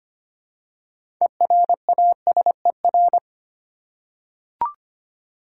Morse code, heard as a keyed tone, about 25 wpm, 700 Hz. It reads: ERASER
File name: practice / 25wpm / Easy Words with Hints - 25wpm